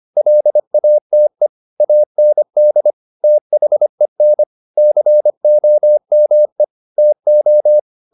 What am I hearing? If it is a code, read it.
Morse code, heard as a keyed tone, about 25 wpm, 600 Hz. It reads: LATE AND THEN COME TO